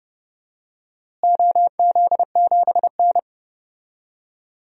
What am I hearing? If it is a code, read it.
Morse code, heard as a keyed tone, about 30 wpm, 700 Hz. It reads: OZ7N
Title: DXCC-100 Call Signs - 4 Characters - Repeated 3x - 30wpm